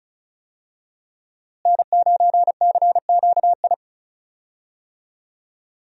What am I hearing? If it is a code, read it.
Morse code, heard as a keyed tone, about 36 wpm, 700 Hz. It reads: N9CQI
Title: USA Call Signs - 5 Characters - Repeated 2x - 35wpm